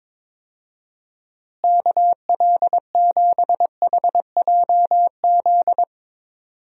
Morse code, heard as a keyed tone, about 22 wpm, 700 Hz. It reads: KL7HJZ
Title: DXCC-100 Call Signs - 6 Characters - Repeated 2x - 22wpm